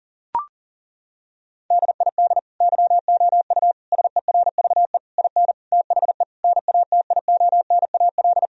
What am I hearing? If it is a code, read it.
Morse code, heard as a keyed tone, about 37 wpm, 700 Hz. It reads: DID YOU SERVE IN THE NATIONAL
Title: Sentences from Top 1000 Words - Review Part 4 - 40wpm